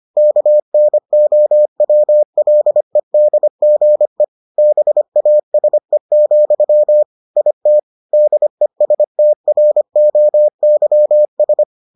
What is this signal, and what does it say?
Morse code, heard as a keyed tone, about 25 wpm, 600 Hz. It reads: KNOWLEDGE BASE, IT DESTROYS